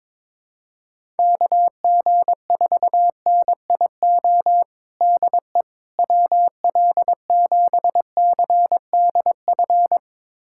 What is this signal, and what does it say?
Morse code, heard as a keyed tone, about 22 wpm, 700 Hz. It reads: KG4NIO DE WL7CDF
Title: USA Callsign DE Callsign - 6 Characters - 22wpm